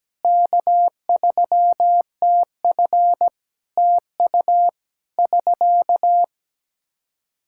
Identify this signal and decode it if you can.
Morse code, heard as a keyed tone, about 17 wpm, 700 Hz. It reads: K3TF TU <SK>
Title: Practice QSO - Exchange 4 - Ending - 17wpm